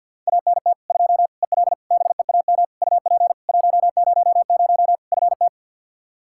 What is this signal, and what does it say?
Morse code, heard as a keyed tone, about 48 wpm, 700 Hz. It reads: ANT 2 EL BEAM UP 100 FT
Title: K1USN SST QSO Practice 50wpm